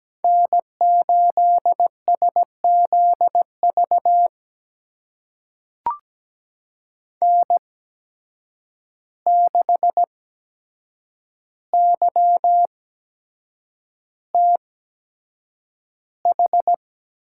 Morse code, heard as a keyed tone, about 17 wpm, spaced as for 7 wpm, 700 Hz. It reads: N8SZV N6YTH
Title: USA Call Signs - 5 Characters - 8x Character Spacing - 17wpm